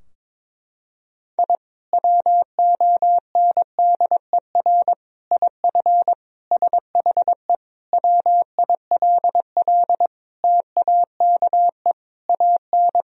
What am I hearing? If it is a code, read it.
Morse code, heard as a keyed tone, about 22 wpm, 700 Hz. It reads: I WONDER IF SHE WILL TAKE AN
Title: Sentences from Top 1000 Words - Review Part 3 - 22wpm